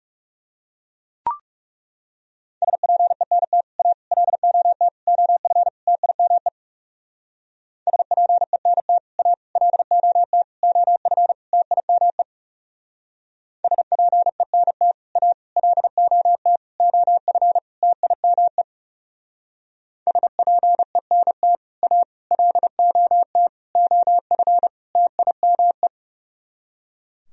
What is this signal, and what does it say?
Morse code, heard as a keyed tone, about 34 wpm, 700 Hz. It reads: SPENT A LOT OF TIME SPENT A LOT OF TIME SPENT A LOT OF TIME SPENT A LOT OF TIME
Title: Speed-Racing - Sets of 5 Words 30wpm